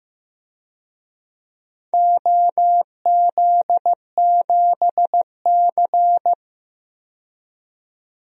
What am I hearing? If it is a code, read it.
Morse code, heard as a keyed tone, about 16 wpm, 700 Hz. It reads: OZ7C
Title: DXCC-100 Call Signs - 4 Characters - 15wpm